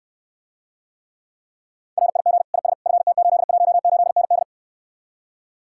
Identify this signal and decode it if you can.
Morse code, heard as a keyed tone, about 69 wpm, 700 Hz. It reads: RIG IS FT818ND